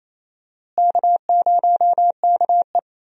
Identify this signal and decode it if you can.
Morse code, heard as a keyed tone, about 28 wpm, 700 Hz. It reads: K0KE
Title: USA Call Signs - 4 Characters - Repeated 2x - 28wpm